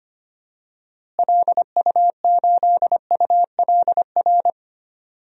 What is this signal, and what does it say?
Morse code, heard as a keyed tone, about 25 wpm, 700 Hz. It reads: LU8ULR
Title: DXCC-100 Call Signs - 6 Characters - Repeated 3x - 25wpm